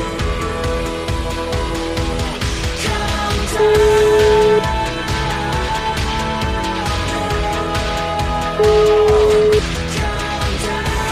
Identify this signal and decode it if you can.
Ringback tone (Europe, 425 Hz), with noise.